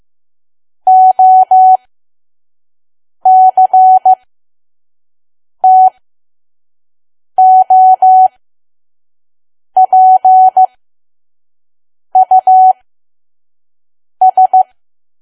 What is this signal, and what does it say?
Morse code, heard as a keyed tone, about 15 wpm, spaced as for 5 wpm, 750 Hz. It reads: OCTOPUS
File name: Archive / 5 WPM files / PM